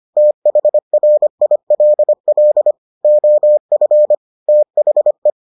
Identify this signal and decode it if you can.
Morse code, heard as a keyed tone, about 25 wpm, 600 Hz. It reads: THRILL OF THE